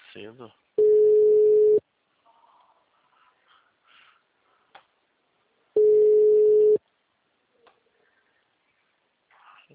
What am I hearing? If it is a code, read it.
Ringback tone (Europe, 425 Hz), with speech.
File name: out- -2020- - - .2